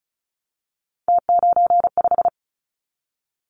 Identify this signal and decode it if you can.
Morse code, heard as a keyed tone, about 36 wpm, 700 Hz. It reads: T95
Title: Callsign Prefixes for Top 100 DXCC - 35wpm